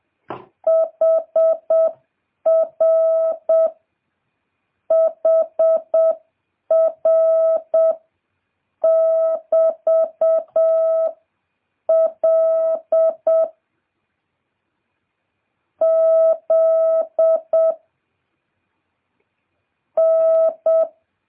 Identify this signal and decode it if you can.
Morse code, heard as a keyed tone, about 7 wpm, 650 Hz. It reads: HR HR <BT> L Z N